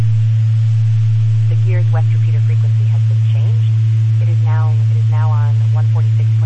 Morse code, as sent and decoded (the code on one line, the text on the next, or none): none